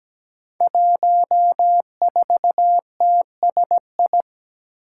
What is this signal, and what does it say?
Morse code, heard as a keyed tone, about 17 wpm, 700 Hz. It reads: 14TSI
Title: Five Challenging Letter-Number Combinations 17wpm